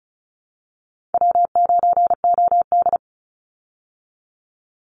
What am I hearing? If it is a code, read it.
Morse code, heard as a keyed tone, about 34 wpm, 700 Hz. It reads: W9OD
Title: Call Signs Repeated 3x - 35wpm